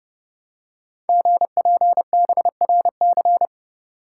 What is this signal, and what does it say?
Morse code, heard as a keyed tone, about 30 wpm, 700 Hz. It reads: GPBRC
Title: Code Groups - 5 Letters - 30wpm